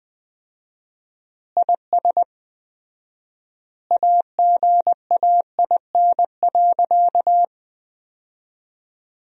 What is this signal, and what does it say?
Morse code, heard as a keyed tone, about 21 wpm, 700 Hz. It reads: IS AGAIN.
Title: 4x Word-Spacing - Sentences from Top 200 Words - 20wpm